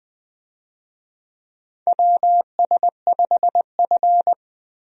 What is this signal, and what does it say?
Morse code, heard as a keyed tone, about 20 wpm, 700 Hz. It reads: WS5F
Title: USA Call Signs - 4 Characters - 20wpm